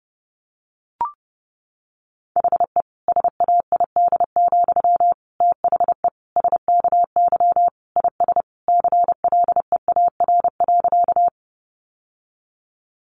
Morse code, heard as a keyed tone, about 28 wpm, 700 Hz. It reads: HE SAID, THE SKY IS CLEAR.
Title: Sentences from Top 1200 Words 30wpm